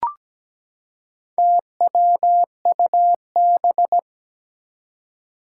Morse code, heard as a keyed tone, about 17 wpm, 700 Hz. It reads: TWUB
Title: Four Challenging Letter-Number Combinations 17wpm